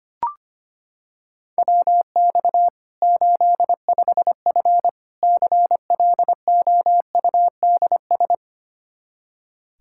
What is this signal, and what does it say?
Morse code, heard as a keyed tone, about 25 wpm, 700 Hz. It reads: WX 85F CLOUDS
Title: Practice QSO - Exchange 2 - Info - 25wpm